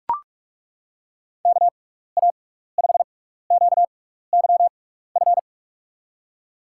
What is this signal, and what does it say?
Morse code, heard as a keyed tone, about 42 wpm, spaced as for 15 wpm, 700 Hz. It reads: KA5QYF
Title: USA Call Signs - 6 Characters - 6x Character Spacing - 45wpm